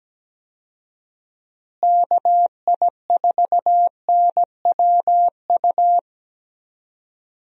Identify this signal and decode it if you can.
Morse code, heard as a keyed tone, about 17 wpm, 700 Hz. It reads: KI4NWU
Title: USA Call Signs - 6 Characters - Repeated 3x - 17wpm